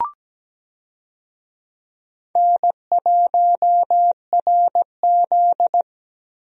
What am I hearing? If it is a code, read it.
Morse code, heard as a keyed tone, about 17 wpm, 700 Hz. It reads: N1RZ
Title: USA Call Signs - 4 Characters - Repeated 3x - 17wpm